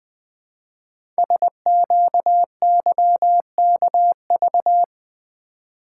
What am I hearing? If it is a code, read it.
Morse code, heard as a keyed tone, about 20 wpm, 700 Hz. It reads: SQYKV
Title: Code Groups - 5 Letters - 20wpm